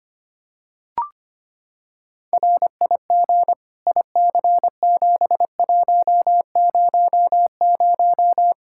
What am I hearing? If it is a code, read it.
Morse code, heard as a keyed tone, about 25 wpm, 700 Hz. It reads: RIG IC7100
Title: Practice QSO - Exchange 2 - Info - 25wpm